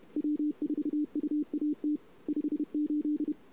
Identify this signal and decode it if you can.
Morse code, heard as a keyed tone, about 31 wpm, 300 Hz. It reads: W4UAT 58
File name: CW / 2008-ss-cw-k5tr-op-n5rz -23:00